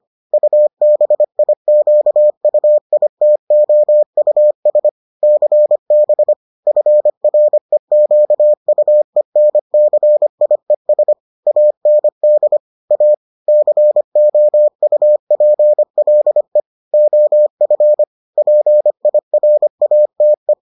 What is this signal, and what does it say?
Morse code, heard as a keyed tone, about 25 wpm, 600 Hz. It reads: UBIQUITOUS CB FREQUENCIES AND A COUPLE OF PIRATE